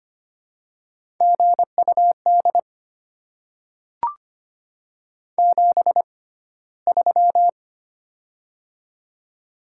Morse code, heard as a keyed tone, about 25 wpm, spaced as for 15 wpm, 700 Hz. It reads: GUD 73
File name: Top 20 QSO Elements - 6x Character Spacing - 25wpm